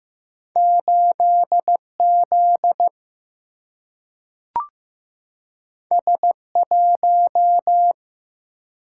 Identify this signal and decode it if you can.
Morse code, heard as a keyed tone, about 15 wpm, 700 Hz. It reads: 8Z S1
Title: Single-Letter Single-Number 15wpm